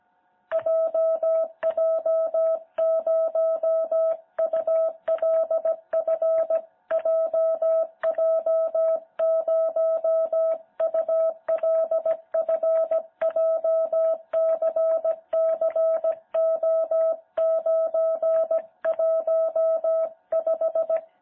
Morse code, heard as a keyed tone, about 17 wpm, 660 Hz. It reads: JJ0ULF JJ0ULF JCCO915
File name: jh8cbh / 2021cw